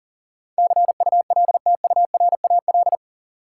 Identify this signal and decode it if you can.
Morse code, heard as a keyed tone, about 40 wpm, 700 Hz. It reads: CULTURAL
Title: Sentences from Top 1500 Words - New Word Drill - 40wpm